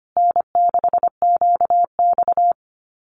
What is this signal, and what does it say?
Morse code, heard as a keyed tone, about 25 wpm, 700 Hz. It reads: N6QX
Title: Common QSO Phrases Encore 25wpm